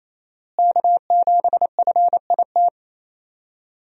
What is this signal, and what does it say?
Morse code, heard as a keyed tone, about 28 wpm, 700 Hz. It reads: K7FIT
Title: USA Call Signs - 5 Characters - Repeated 3x - 28wpm